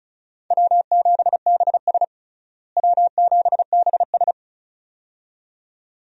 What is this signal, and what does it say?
Morse code, heard as a keyed tone, about 34 wpm, 700 Hz. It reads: W7BS W7BS
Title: USA Call Signs - 4 Characters - Repeated 2x - 35wpm